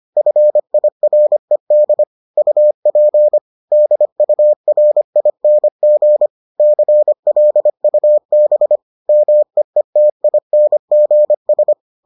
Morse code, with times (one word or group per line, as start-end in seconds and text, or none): 0.2-2.0 FIRED
2.4-3.4 UP
3.7-6.3 DURING
6.6-8.8 CLUB
9.1-11.7 MEETINGS